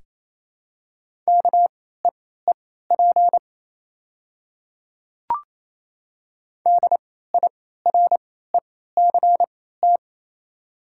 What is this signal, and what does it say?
Morse code, heard as a keyed tone, about 28 wpm, spaced as for 16 wpm, 690 Hz. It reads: KEEP DIRECT
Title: 3x Character Spacing - Top 500 Words - 28wpm